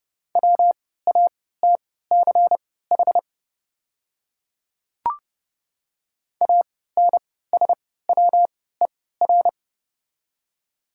Morse code, heard as a keyed tone, about 30 wpm, spaced as for 17 wpm, 690 Hz. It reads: WATCH ANSWER
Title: 3x Character Spacing - Top 300 Words - 30wpm